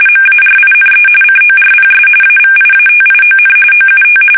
RTTY, about 45.45 baud. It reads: MARINE CORPS AVIATION OUT
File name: fsk850 / usnavy5